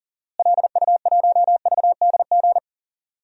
Morse code, heard as a keyed tone, about 40 wpm, 700 Hz. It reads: LU1VDG
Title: DXCC-100 Call Signs - 6 Characters - Repeated 2x - 40wpm